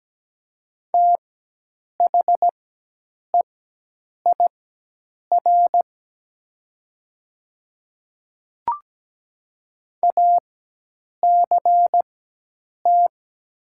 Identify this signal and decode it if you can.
Morse code, heard as a keyed tone, about 17 wpm, spaced as for 8 wpm, 700 Hz. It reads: THEIR ACT
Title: 4x Character Spacing - Top 200 Words - 17wpm